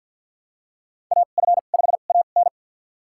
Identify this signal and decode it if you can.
Morse code, heard as a keyed tone, about 46 wpm, spaced as for 34 wpm, 700 Hz. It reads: AF5AN
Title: USA Call Signs - 5 Characters - 2x Character Spacing - 50wpm